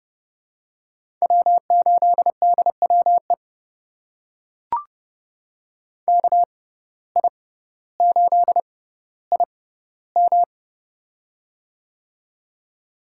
Morse code, heard as a keyed tone, about 29 wpm, spaced as for 15 wpm, 700 Hz. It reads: W8DWE KI8IM
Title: USA Call Signs - 5 Characters - 6x Character Spacing - 30wpm